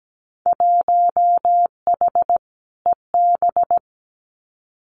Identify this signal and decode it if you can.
Morse code, heard as a keyed tone, about 17 wpm, 700 Hz. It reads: 1H EB
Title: Winter Field Day Exchanges 17wpm